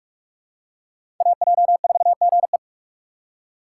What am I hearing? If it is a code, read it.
Morse code, heard as a keyed tone, about 42 wpm, 700 Hz. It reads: AJ4GE
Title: USA Call Signs - 5 Characters - 4x Character Spacing - 45wpm